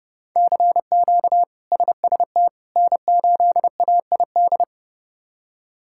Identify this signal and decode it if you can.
Morse code, heard as a keyed tone, about 30 wpm, 700 Hz. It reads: CQ SST N8AID
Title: K1USN SST QSO Practice - 30wpm